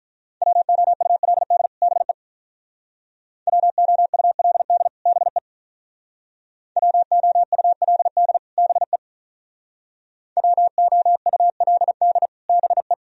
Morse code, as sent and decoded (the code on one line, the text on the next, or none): .-- --- ..- .-.. -.. / -... / .-- --- ..- .-.. -.. / -... / .-- --- ..- .-.. -.. / -... / .-- --- ..- .-.. -.. / -... .
WOULD BE WOULD BE WOULD BE WOULD BE